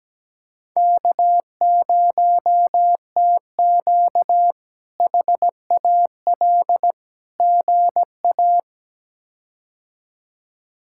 Morse code, heard as a keyed tone, about 17 wpm, 700 Hz. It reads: K0TQ HAL GA
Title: K1USN SST QSO Practice - 17wpm